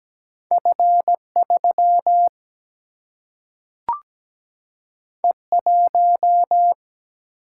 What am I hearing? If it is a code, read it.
Morse code, heard as a keyed tone, about 17 wpm, 700 Hz. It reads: F3 E1
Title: Single-Letter Single-Number 17wpm